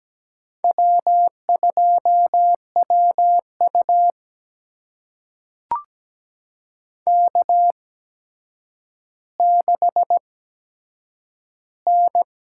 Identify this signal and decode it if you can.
Morse code, heard as a keyed tone, about 17 wpm, spaced as for 8 wpm, 700 Hz. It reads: W2WU K6N
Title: USA Call Signs - 4 Characters - 8x Character Spacing - 17wpm